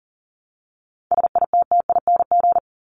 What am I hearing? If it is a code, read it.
Morse code, heard as a keyed tone, about 40 wpm, 700 Hz. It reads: SITTING
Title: ING Suffix - Top 100 Words with Base Words 40wpm